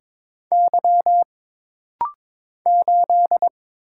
Morse code, heard as a keyed tone, about 23 wpm, 700 Hz. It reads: Y 8
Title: Single Letter-Number - Rapid-Fire 22wpm